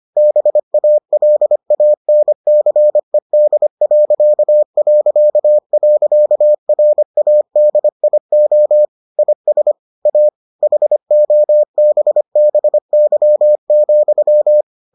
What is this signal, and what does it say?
Morse code, heard as a keyed tone, about 25 wpm, 600 Hz. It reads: BALANCED...RADIO IS A HOBBY,